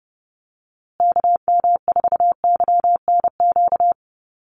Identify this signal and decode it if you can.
Morse code, heard as a keyed tone, about 30 wpm, 700 Hz. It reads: KM4YNQ
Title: USA Call Signs Repeated 2x 30wpm